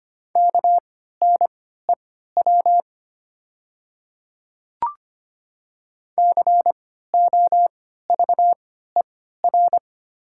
Morse code, heard as a keyed tone, about 25 wpm, spaced as for 14 wpm, 690 Hz. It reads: KNEW COVER
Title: 3x Character Spacing - Top 500 Words - 25wpm